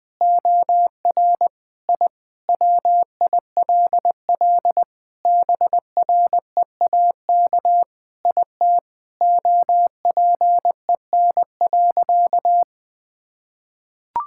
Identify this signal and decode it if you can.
Morse code, heard as a keyed tone, about 20 wpm, 700 Hz. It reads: OR I WILL BREAK IT OPEN.
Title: Sentences from Top 1000 Words - Review Part 6 - 20wpm